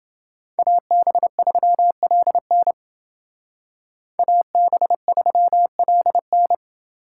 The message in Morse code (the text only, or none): AB3LN AB3LN